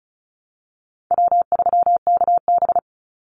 Morse code, heard as a keyed tone, about 34 wpm, 700 Hz. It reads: W3KB
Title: Easy Call Signs Repeated 3x - 35wpm